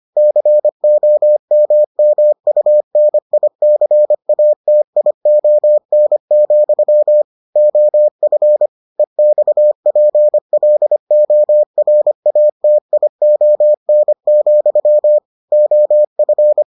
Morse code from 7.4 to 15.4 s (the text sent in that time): OF EXPLORATION,